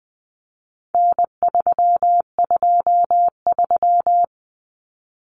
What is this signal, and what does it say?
Morse code, heard as a keyed tone, about 20 wpm, 700 Hz. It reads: N323
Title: Common Cut Numbers - 4-Digit 20wpm